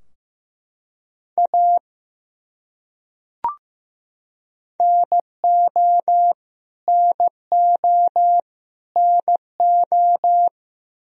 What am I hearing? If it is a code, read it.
Morse code, heard as a keyed tone, about 15 wpm, 700 Hz. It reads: A NO NO NO
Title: Top 100 Words Repeated 15wpm